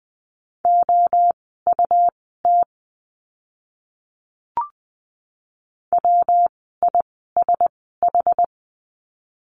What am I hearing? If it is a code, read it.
Morse code, heard as a keyed tone, about 20 wpm, spaced as for 14 wpm, 700 Hz. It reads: OUT WISH
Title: 115 Lesson 19f - Letter w - Words 20wpm